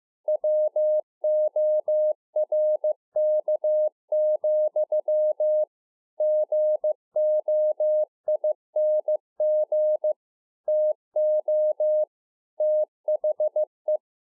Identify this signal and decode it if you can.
Morse code, heard as a keyed tone, about 15 wpm, 610 Hz. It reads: WORK, GOING TO THE
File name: 15WPM